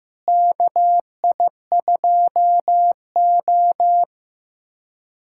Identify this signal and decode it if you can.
Morse code, heard as a keyed tone, about 16 wpm, 700 Hz. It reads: KI2O
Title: USA Callsign DE Callsign - 4 Characters - 15wpm